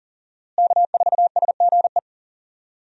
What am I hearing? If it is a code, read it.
Morse code, heard as a keyed tone, about 37 wpm, 700 Hz. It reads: K4SGE
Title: USA Call Signs - 5 Characters - Repeated 3x - 40wpm